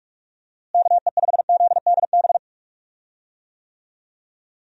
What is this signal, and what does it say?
Morse code, heard as a keyed tone, about 46 wpm, 700 Hz. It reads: KE5ZDB
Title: USA Call Signs - 6 Characters - Repeated 2x - 45wpm